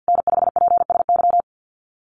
Morse code, heard as a keyed tone, about 50 wpm, 700 Hz. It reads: N5PSY
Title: USA Call Signs Repeated 2x 50wpm